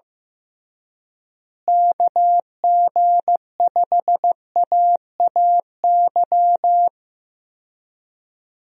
Morse code, heard as a keyed tone, about 15 wpm, 700 Hz. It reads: KG5AAY
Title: USA Call Signs - 6 Characters - Repeated 3x - 15wpm